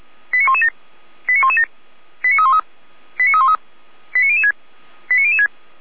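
Five-tone selective call (ZVEI), heard as quick runs of five tones, tones of about 70 ms. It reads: 89117 89117 89313 89313 89006 89006
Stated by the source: ZVEI five-tone selective call 89265 received by SDR.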